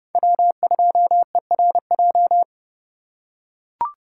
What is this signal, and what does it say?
Morse code, heard as a keyed tone, about 29 wpm, 700 Hz. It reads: W2ERJ